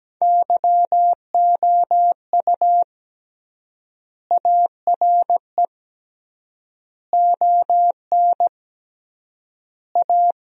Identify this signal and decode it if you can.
Morse code, heard as a keyed tone, about 17 wpm, 700 Hz. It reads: YOU ARE ON A
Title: 3x Word-Spacing - Sentences from Top 400 Words - 17wpm